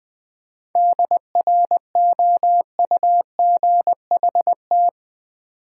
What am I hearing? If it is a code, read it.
Morse code, heard as a keyed tone, about 20 wpm, 700 Hz. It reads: DROUGHT
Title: 7 Letter Words - First 3 Letters Given - 20wpm